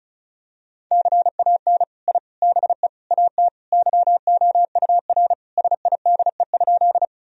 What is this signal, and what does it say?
Morse code, heard as a keyed tone, about 36 wpm, 700 Hz. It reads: CAN I BE AT YOUR SIDE?
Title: Sentences from Top 1000 Words - Review Part 1 - 35wpm